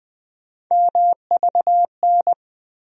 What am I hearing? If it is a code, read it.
Morse code, heard as a keyed tone, about 20 wpm, 700 Hz. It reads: MVN
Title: USA Call Sign Postfixes - 3 Letters 20wpm